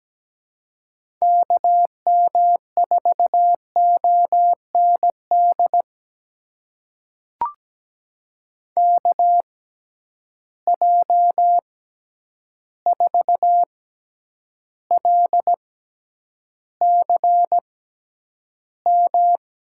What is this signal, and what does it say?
Morse code, heard as a keyed tone, about 17 wpm, spaced as for 9 wpm, 700 Hz. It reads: KM4OND KJ4LCM
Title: USA Call Signs - 6 Characters - 6x Character Spacing - 17wpm